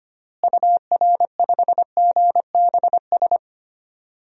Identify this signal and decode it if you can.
Morse code, heard as a keyed tone, about 25 wpm, 700 Hz. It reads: UR5GBS
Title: DXCC-100 Call Signs - 6 Characters - Repeated 2x - 25wpm